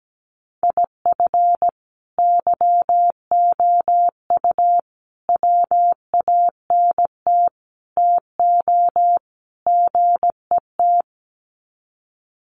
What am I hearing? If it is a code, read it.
Morse code, heard as a keyed tone, about 17 wpm, 700 Hz. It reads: IF YOU WANT TO GET
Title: Sets of 5 Words Encore 17wpm